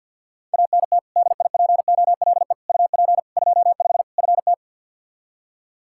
Morse code, heard as a keyed tone, about 46 wpm, 700 Hz. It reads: ANT DIPOLE UP 25 FT
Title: Practice QSO - Exchange 2 - Info - 50wpm